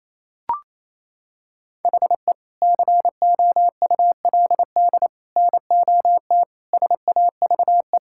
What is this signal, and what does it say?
Morse code, heard as a keyed tone, about 28 wpm, 700 Hz. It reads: HE COULD NOT SAVE
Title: Sentences from Top 1600 Words 28wpm